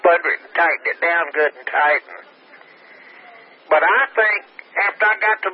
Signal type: Emergency Alert System (SAME) tones